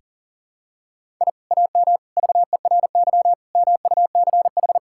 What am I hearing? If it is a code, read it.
Morse code, heard as a keyed tone, about 37 wpm, 700 Hz. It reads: I AM VERY MUCH
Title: Sentences from Top 1000 Words - Review Part 6 - 40wpm